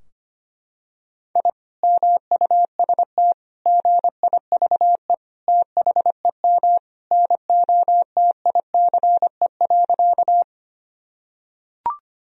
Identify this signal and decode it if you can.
Morse code, heard as a keyed tone, about 25 wpm, 700 Hz. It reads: I MUST GIVE THEM NOTICE.
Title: Sentences from Top 1000 Words - Review Part 2 - 25wpm